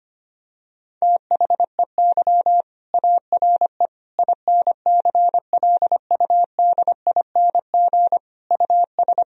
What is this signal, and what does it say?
Morse code, heard as a keyed tone, about 25 wpm, 700 Hz. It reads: THEY ARE INCLUDING US
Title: Sentences from Top 1000 Words - Review Part 4 - 25wpm